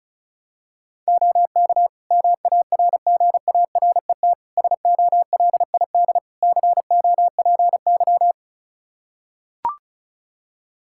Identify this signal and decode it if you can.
Morse code, heard as a keyed tone, about 36 wpm, 700 Hz. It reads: OK MARGARET SOLID COPY
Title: Practice QSO - Exchange 3 - More Info - 35wpm